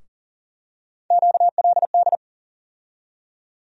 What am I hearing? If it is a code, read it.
Morse code, heard as a keyed tone, about 40 wpm, 700 Hz. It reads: QLD
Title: USA Call Sign Postfixes - 3 Letters 40wpm